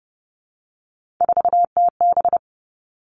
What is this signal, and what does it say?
Morse code, heard as a keyed tone, about 30 wpm, 700 Hz. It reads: 4TB
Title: All Cut Numbers - 3-Digit 30wpm